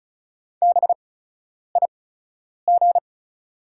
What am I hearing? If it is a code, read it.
Morse code, heard as a keyed tone, about 36 wpm, spaced as for 10 wpm, 690 Hz. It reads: BIG
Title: 8x Character Spacing - Top 300 Words - 35wpm